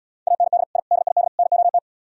Morse code, heard as a keyed tone, about 74 wpm, 700 Hz. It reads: AND I DID NOT